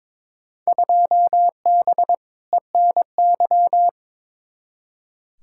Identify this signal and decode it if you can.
Morse code, heard as a keyed tone, about 22 wpm, 700 Hz. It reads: 2B ENY
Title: Field Day Exchanges Encore - 22wpm